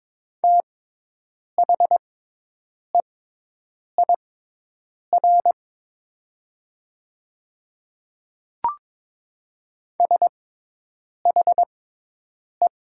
Morse code, heard as a keyed tone, about 22 wpm, spaced as for 8 wpm, 700 Hz. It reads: THEIR SHE
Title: 6x Character Spacing - Top 100 Words - 22wpm